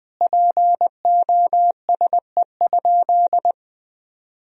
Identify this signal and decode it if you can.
Morse code, heard as a keyed tone, about 20 wpm, 700 Hz. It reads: POSE?
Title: Sentences from Top 1000 Words - Review Part 2 - 20wpm